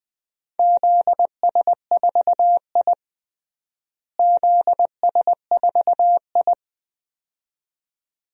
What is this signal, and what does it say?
Morse code, heard as a keyed tone, about 21 wpm, 700 Hz. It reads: ZS4I ZS4I
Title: DXCC-100 Call Signs - 4 Characters - Repeated 2x - 20wpm